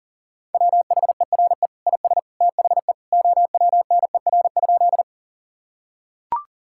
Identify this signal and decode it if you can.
Morse code, heard as a keyed tone, about 40 wpm, 700 Hz. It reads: WHERE IS THE OWNER?
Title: Sentences from Top 1600 Words 40wpm